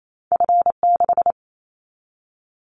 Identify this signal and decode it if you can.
Morse code, heard as a keyed tone, about 28 wpm, 700 Hz. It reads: F6
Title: Callsign Prefixes for Top 50 DXCC - 28wpm